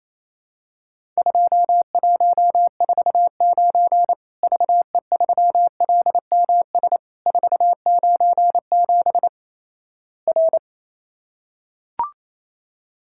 Morse code, heard as a keyed tone, about 28 wpm, 700 Hz. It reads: 2149 VE3LMS 497 R
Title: Worked All Europe DX Contest - QTC Exchange - 28wpm